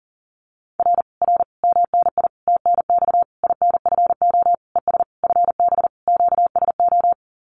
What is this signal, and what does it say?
Morse code, heard as a keyed tone, about 40 wpm, 700 Hz. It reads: R R MNI TNX INFO ES FB QSO
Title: Common QSO Phrases Encore 40wpm